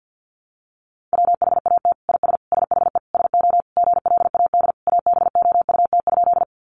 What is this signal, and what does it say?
Morse code, heard as a keyed tone, about 50 wpm, 700 Hz. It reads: WHAT IS SHE SO GLAD ABOUT?
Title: Sentences from Top 800 Words 50wpm